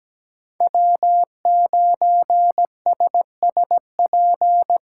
Morse code, heard as a keyed tone, about 18 wpm, 700 Hz. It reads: W9SSP
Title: Speed Racing - USA 1x3 Call Signs 17wpm